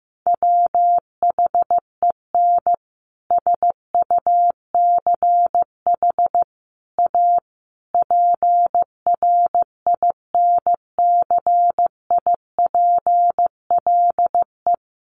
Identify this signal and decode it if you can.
Morse code, heard as a keyed tone, about 15 wpm, 700 Hz. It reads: WHEN SUCH A PRINCIPLE
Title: Sentences from Top 1400 Words 15wpm